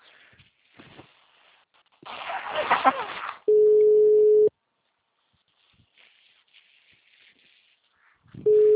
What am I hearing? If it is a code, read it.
Ringback tone (Europe, 425 Hz).
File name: out- -2007- - - .800